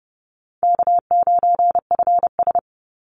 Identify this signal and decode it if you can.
Morse code, heard as a keyed tone, about 29 wpm, 700 Hz. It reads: K9FS
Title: Easy Call Signs Repeated 2x - 30wpm